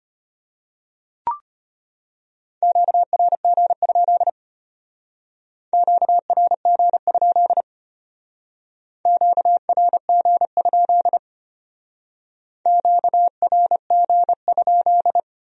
Morse code, heard as a keyed tone, about 31 wpm, 700 Hz. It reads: QRG? QRG? QRG? QRG?
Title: Speed-Racing Q-Codes - 25wpm